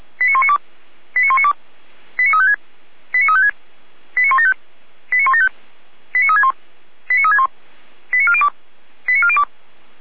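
Five-tone selective call (ZVEI), heard as quick runs of five tones, tones of about 80 ms. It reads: 89192 89192 89356 89356 89175 89175 89361 89361 89402 89402
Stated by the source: ZVEI five-tone call 89075 received by SDR.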